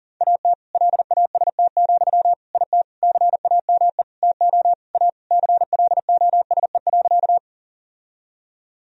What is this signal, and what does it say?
Morse code, heard as a keyed tone, about 40 wpm, 700 Hz. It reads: AT LAST, IT CAME TO A CLOSE.
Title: Sentences from Top 1000 Words - Review Part 1 - 40wpm